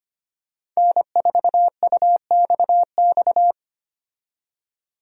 Morse code, heard as a keyed tone, about 25 wpm, 700 Hz. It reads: N4UXX
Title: USA Call Signs - 5 Characters - 25wpm